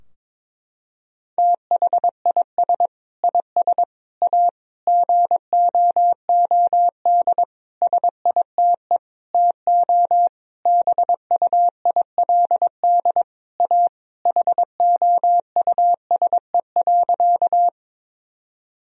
Morse code, heard as a keyed tone, about 22 wpm, 700 Hz. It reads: THIS IS A GOOD SITE TO BUILD A HOUSE.
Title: Sentences from Top 1000 Words - Review Part 4 - 22wpm